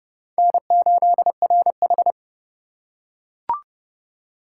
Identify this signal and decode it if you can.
Morse code, heard as a keyed tone, about 27 wpm, 700 Hz. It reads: N8RH